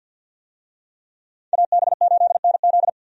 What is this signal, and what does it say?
Morse code, heard as a keyed tone, about 52 wpm, 700 Hz. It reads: AB8NZ
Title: USA Call Signs - 5 Characters - 50wpm